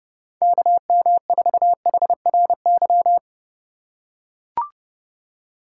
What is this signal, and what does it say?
Morse code, heard as a keyed tone, about 30 wpm, 700 Hz. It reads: KM4HRY
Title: USA Call Signs - 6 Characters - 6x Character Spacing - 30wpm